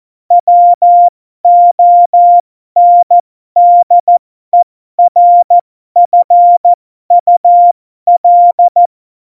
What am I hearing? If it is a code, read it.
Morse code, heard as a keyed tone, about 14 wpm, spaced as for 12 wpm, 700 Hz. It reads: WONDERFUL
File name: Morsekurs / Witze